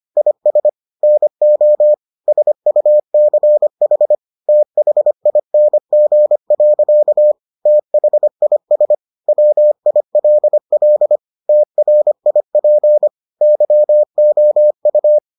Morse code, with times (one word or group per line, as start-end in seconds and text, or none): 0.2-0.7 IS
1.0-1.9 NO
2.3-4.2 SUCH
4.5-7.3 THING.
7.7-8.9 THIS
9.3-11.2 WILL
11.5-13.1 TRIP
13.4-15.2 YOU